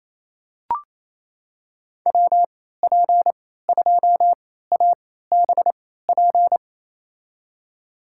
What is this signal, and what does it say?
Morse code, heard as a keyed tone, about 28 wpm, spaced as for 16 wpm, 700 Hz. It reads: WP2ABP
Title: USA Call Signs - 6 Characters - 3x Character Spacing - 28wpm